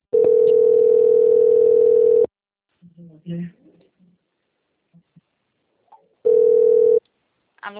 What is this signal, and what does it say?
Ringback tone (North America), 440 + 480 Hz, with speech.
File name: out- - - - - .95773